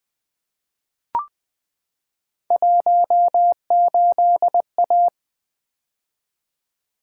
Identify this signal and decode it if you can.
Morse code, heard as a keyed tone, about 20 wpm, 700 Hz. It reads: 18A
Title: Three Letter-Number Combinations - 20wpm